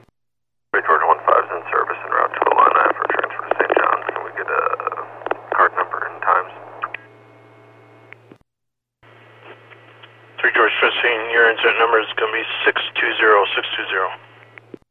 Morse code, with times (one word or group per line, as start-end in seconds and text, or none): none